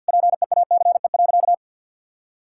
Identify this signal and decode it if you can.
Morse code, heard as a keyed tone, about 50 wpm, 700 Hz. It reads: PEACE.
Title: Sentences from Top 1600 Words 50wpm